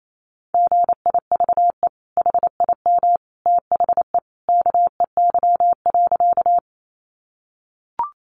Morse code, heard as a keyed tone, about 28 wpm, 700 Hz. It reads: GIVE HIM THE KEY.
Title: Sentences from Top 600 Words 28wpm